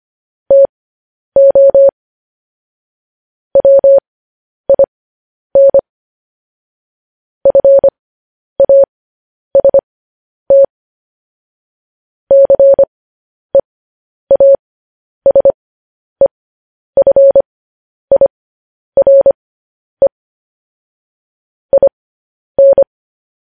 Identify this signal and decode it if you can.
Morse code, heard as a keyed tone, about 25 wpm, spaced as for 10 wpm, 550 Hz. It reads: TO WIN FAST CEASEFIRE IN
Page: News Headlines in Morse Code at 10 WPM